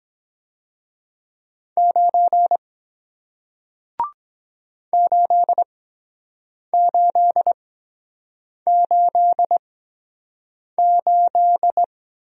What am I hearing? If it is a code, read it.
Morse code, heard as a keyed tone, about 23 wpm, spaced as for 7 wpm, 700 Hz. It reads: 9 8888